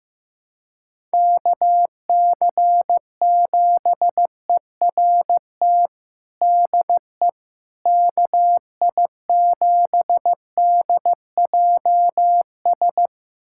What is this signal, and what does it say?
Morse code, heard as a keyed tone, about 15 wpm, 700 Hz. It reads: KC7ERT DE KI7DJS
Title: USA Callsign DE Callsign - 6 Characters - 15wpm